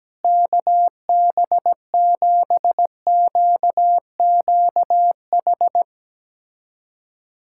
Morse code, heard as a keyed tone, about 17 wpm, 700 Hz. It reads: KB7QQH
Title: USA Callsign DE Callsign 17wpm